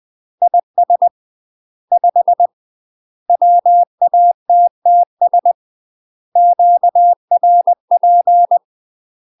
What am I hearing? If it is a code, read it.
Morse code, heard as a keyed tone, about 20 wpm, 700 Hz. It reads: IS 5 WATTS QRP
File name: UK MASTER QSOs TRAINING / 20WPM / UK MASTER QSOs 0072